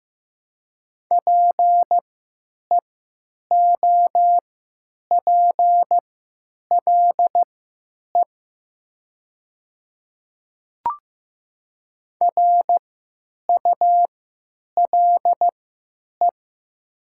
Morse code, heard as a keyed tone, about 15 wpm, spaced as for 9 wpm, 700 Hz. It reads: PEOPLE RULE
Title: 3x Character Spacing - Top 500 Words - 15wpm